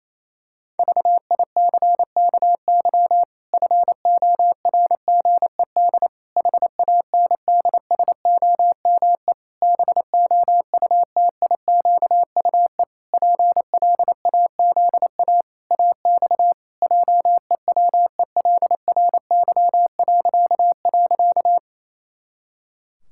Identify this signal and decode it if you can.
Morse code, heard as a keyed tone, about 28 wpm, 700 Hz. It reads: VICKY FORGED HANDSOME BOUTIQUE PLAZA AX JEWELRY..
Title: Pangram Challenge v2 - 28wpm